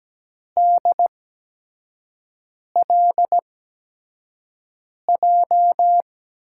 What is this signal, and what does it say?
Morse code, heard as a keyed tone, about 17 wpm, spaced as for 5 wpm, 700 Hz. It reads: DLJ
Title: USA Call Signs - 8x Character Spacing - 17wpm